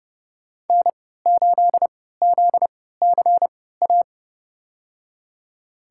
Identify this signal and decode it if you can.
Morse code, heard as a keyed tone, about 31 wpm, spaced as for 17 wpm, 700 Hz. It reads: N8ZCA
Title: USA Call Signs - 5 Characters - 3x Character Spacing - 30wpm